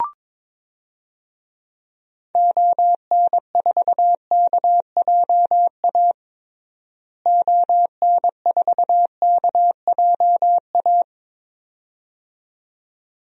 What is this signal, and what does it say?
Morse code, heard as a keyed tone, about 23 wpm, 700 Hz. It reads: ON4KJA ON4KJA
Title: DXCC-100 Call Signs - 6 Characters - Repeated 2x - 22wpm